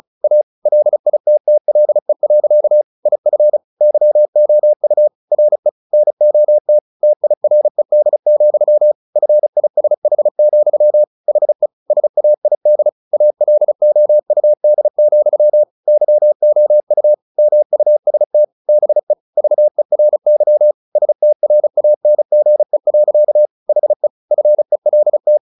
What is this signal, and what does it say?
Morse code, heard as a keyed tone, about 34 wpm, 600 Hz. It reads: A LITTLE. IF YOU RE NOT TIRED, FISH, HE SAID ALOUD, YOU MUST BE VERY STRANGE. HE FELT
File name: the old man and the sea / the old man and the sea 1 35wpm